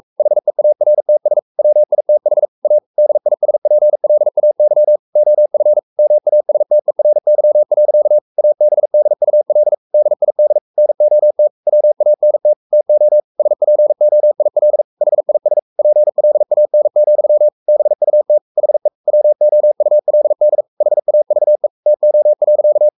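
Morse code, heard as a keyed tone, about 44 wpm, 600 Hz. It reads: HEARTS WITH A DISPLAY OF MASTERY. ABDUL DID NOT WANT TO SPOIL HIS PLAN, BUT HE WOULD HAVE TO.
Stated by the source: Morse code at 43 wpm sending Ken Follett's novel Never in English, from CW med Gnister.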